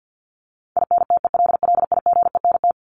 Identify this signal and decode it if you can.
Morse code, heard as a keyed tone, about 48 wpm, 700 Hz. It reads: INTELLIGENT
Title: Top 751-1000 Words 50wpm